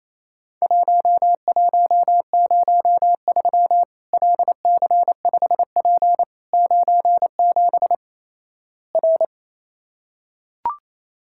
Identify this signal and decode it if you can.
Morse code, heard as a keyed tone, about 28 wpm, 700 Hz. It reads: 1103 LC5P 97 R
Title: Worked All Europe DX Contest - QTC Exchange - 28wpm